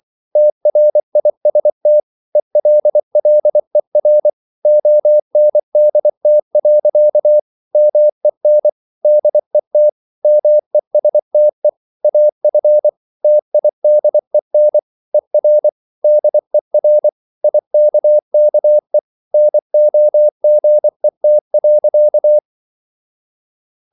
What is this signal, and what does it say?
Morse code, heard as a keyed tone, about 24 wpm, 600 Hz. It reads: TRIST ELLER ONDT. MEN DET MESTE AF TIDEN ER DER IKKE NOGET.
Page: Ferie 24 wpm